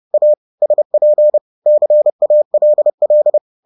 Morse code, heard as a keyed tone, about 30 wpm, 600 Hz. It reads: A SP CALL